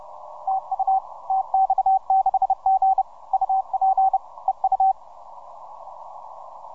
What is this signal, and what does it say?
Morse code, heard as a keyed tone, about 31 wpm, 770 Hz. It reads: TU TX6G UP EU